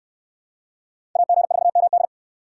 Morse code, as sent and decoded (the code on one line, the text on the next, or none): .- --. ....- -- -..
AG4MD